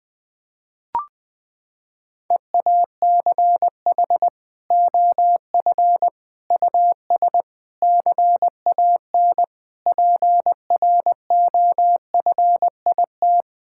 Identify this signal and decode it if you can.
Morse code, heard as a keyed tone, about 20 wpm, 700 Hz. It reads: EACH OF US CAN PROFIT
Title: Sentences from Top 1000 Words - Review Part 5 - 20wpm